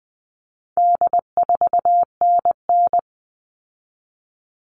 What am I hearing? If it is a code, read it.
Morse code, heard as a keyed tone, about 20 wpm, 700 Hz. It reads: D4NN
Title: All Cut Numbers - 4-Digit 20wpm